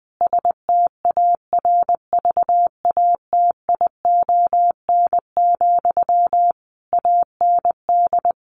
Morse code, heard as a keyed tone, about 20 wpm, 700 Hz. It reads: STARVATION, AND
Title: A Sound Of Thunder 20wpm